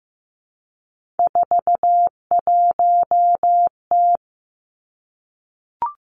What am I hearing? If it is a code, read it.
Morse code, heard as a keyed tone, about 15 wpm, 700 Hz. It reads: 41T
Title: Three Challenging Letter-Number Combinations 15wpm